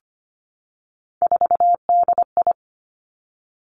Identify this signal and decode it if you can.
Morse code, heard as a keyed tone, about 25 wpm, 700 Hz. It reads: 4DI
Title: Three Challenging Letter-Number Combinations 25wpm